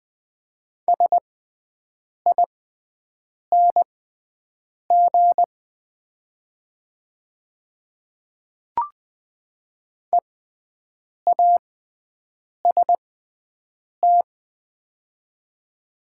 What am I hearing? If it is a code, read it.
Morse code, heard as a keyed tone, about 20 wpm, spaced as for 7 wpm, 700 Hz. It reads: SING EAST
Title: 6x Character Spacing - Top 500 Words - 20wpm